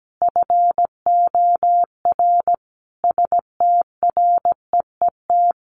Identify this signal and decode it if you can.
Morse code, heard as a keyed tone, about 17 wpm, 700 Hz. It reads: FOR STREET
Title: Sentences from Top 1500 Words 17wpm